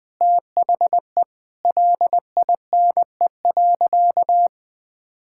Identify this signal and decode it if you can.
Morse code, heard as a keyed tone, about 20 wpm, 700 Hz. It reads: THE LINE.
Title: Sentences with 3-Word Phrases - 20wpm